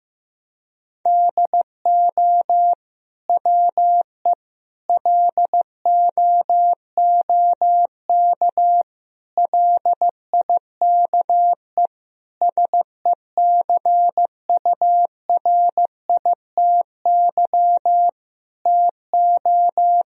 Morse code, heard as a keyed tone, about 15 wpm, 700 Hz. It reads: DO WE LOOK LIKE SECURITY TO
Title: Sentences from Top 1000 Words - Review Part 4 - 15wpm